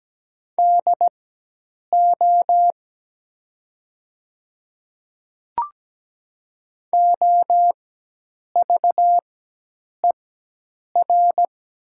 Morse code, heard as a keyed tone, about 17 wpm, spaced as for 8 wpm, 700 Hz. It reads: DO OVER
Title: 4x Character Spacing - Top 100 Words - 17wpm